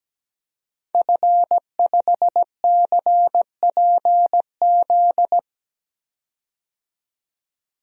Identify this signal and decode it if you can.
Morse code, heard as a keyed tone, about 17 wpm, 700 Hz. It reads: F5CPZ
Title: DXCC-100 Call Signs - 5 Characters - 17wpm